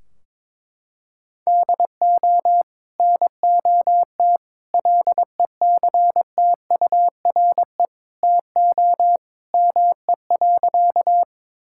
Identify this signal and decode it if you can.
Morse code, heard as a keyed tone, about 22 wpm, 700 Hz. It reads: DO NOT LECTURE TO ME.
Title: Sentences from Top 1600 Words 22wpm